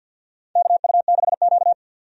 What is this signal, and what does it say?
Morse code, heard as a keyed tone, about 51 wpm, 700 Hz. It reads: KU6Q
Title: USA Call Signs - 4 Characters - 50wpm